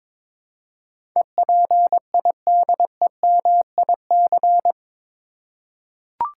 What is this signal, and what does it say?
Morse code, heard as a keyed tone, about 21 wpm, 700 Hz. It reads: EPIDEMIC
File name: practice / 22wpm / Eight Letter Words - 22wpm